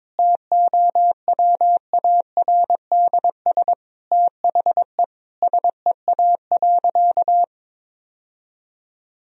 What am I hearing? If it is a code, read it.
Morse code, heard as a keyed tone, about 21 wpm, 700 Hz. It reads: TOWARDS THE SEA.
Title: Sentences from Top 1000 Words - Review Part 2 - 22wpm